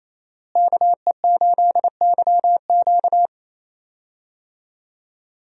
Morse code, heard as a keyed tone, about 28 wpm, 700 Hz. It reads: KE8YQ
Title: USA Call Signs - 5 Characters - Repeated 2x - 28wpm